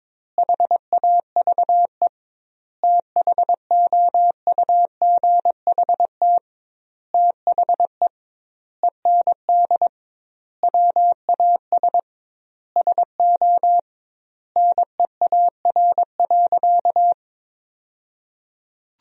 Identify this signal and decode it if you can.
Morse code, heard as a keyed tone, about 22 wpm, 700 Hz. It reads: HAVE THOUGHT THE END WAS SO NEAR.
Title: 2x Word-Spacing - Sentences from Top 300 Words - 22wpm